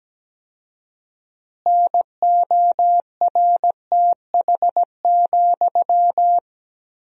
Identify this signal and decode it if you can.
Morse code, heard as a keyed tone, about 17 wpm, 700 Hz. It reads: NORTH,
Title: 4x Word-Spacing - Sentences from Top 400 Words - 17wpm